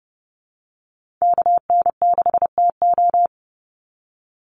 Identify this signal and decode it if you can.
Morse code, heard as a keyed tone, about 30 wpm, 700 Hz. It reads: KN6TO
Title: USA Call Signs 30wpm